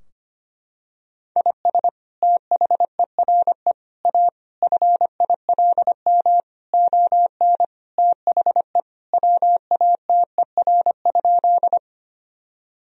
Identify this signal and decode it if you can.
Morse code, heard as a keyed tone, about 25 wpm, 700 Hz. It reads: IS THERE A FILM ON THE WATER?
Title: Sentences from Top 1000 Words - Review Part 3 - 25wpm